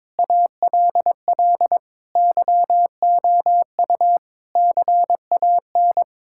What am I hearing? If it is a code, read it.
Morse code, heard as a keyed tone, about 21 wpm, 700 Hz. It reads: ALL YOU CAN
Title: Sentences from Top 1000 Words - Review Part 5 - 22wpm